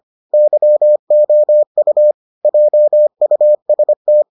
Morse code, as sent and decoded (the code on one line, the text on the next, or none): -.-- --- ..- / .--- ..- ... -
YOU JUST